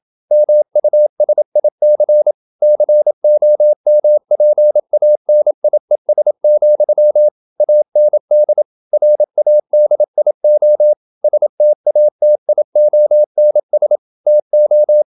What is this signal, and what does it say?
Morse code, heard as a keyed tone, about 27 wpm, 600 Hz. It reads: MUSIC COMPANIES, AND RADIO STATIONS TO
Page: Never 26wpm | CW med Gnister